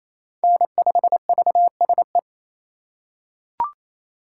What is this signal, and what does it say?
Morse code, heard as a keyed tone, about 28 wpm, 700 Hz. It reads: N5VSE
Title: USA Call Signs - 4x Character Spacing - 28wpm